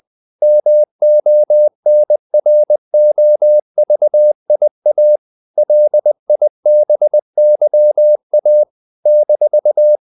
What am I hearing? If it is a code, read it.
Morse code, heard as a keyed tone, about 20 wpm, 600 Hz. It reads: MONROVIA LIBYA -
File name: CW / Practice / NationalCapitals20